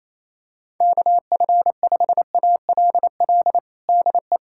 Morse code, heard as a keyed tone, about 28 wpm, 700 Hz. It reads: KF5ALL DE
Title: USA Callsign DE Callsign 28wpm